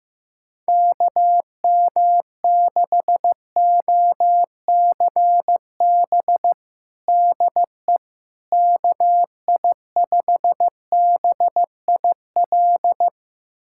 Morse code, heard as a keyed tone, about 15 wpm, 700 Hz. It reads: KM6OCB DE KI5BIL
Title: USA Callsign DE Callsign - 6 Characters - 15wpm